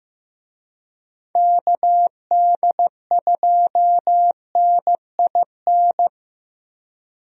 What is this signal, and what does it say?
Morse code, heard as a keyed tone, about 15 wpm, 700 Hz. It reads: KD2NIN
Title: USA Call Signs - 6 Characters - Repeated 3x - 15wpm